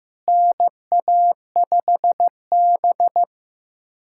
Morse code, heard as a keyed tone, about 15 wpm, 700 Hz. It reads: NA5B
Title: USA Callsign DE Callsign - 4 Characters - 15wpm